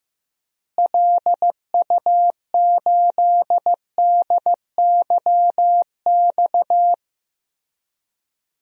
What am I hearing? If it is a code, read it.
Morse code, heard as a keyed tone, about 16 wpm, 700 Hz. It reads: LU8DYX
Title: DXCC-100 Call Signs - 6 Characters - Repeated 3x - 15wpm